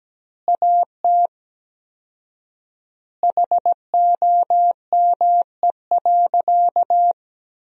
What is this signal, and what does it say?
Morse code, heard as a keyed tone, about 17 wpm, 700 Hz. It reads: AT HOME.
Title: 4x Word-Spacing - Sentences from Top 400 Words - 17wpm